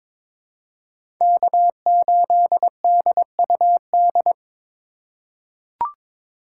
Morse code, heard as a keyed tone, about 21 wpm, 700 Hz. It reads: K8DUD
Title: USA Call Signs - 5 Characters - 2x Character Spacing - 22wpm